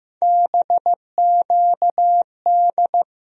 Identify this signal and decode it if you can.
Morse code, heard as a keyed tone, about 15 wpm, 700 Hz. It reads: BQD
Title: USA Call Sign Postfixes - 3 Letters 15wpm